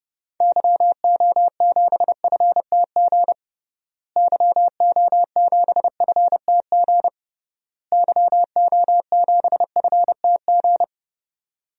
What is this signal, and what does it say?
Morse code, heard as a keyed tone, about 30 wpm, 700 Hz. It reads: YO7FTG YO7FTG YO7FTG
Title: DXCC-100 Call Signs - 6 Characters - Repeated 3x - 30wpm